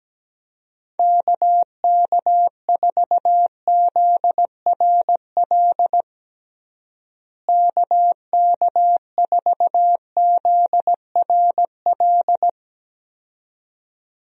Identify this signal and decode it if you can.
Morse code, heard as a keyed tone, about 17 wpm, 700 Hz. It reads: KK4ZRL KK4ZRL
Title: USA Call Signs - 6 Characters - Repeated 2x - 17wpm